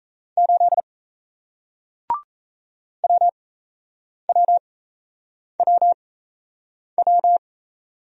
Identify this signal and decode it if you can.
Morse code, heard as a keyed tone, about 38 wpm, spaced as for 8 wpm, 700 Hz. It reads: 8 WWWW